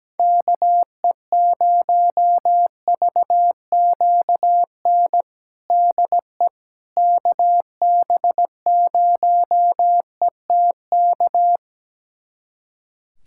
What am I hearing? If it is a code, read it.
Morse code, heard as a keyed tone, about 17 wpm, 700 Hz. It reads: KE0VQN DE KB0ETK
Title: USA Callsign DE Callsign 17wpm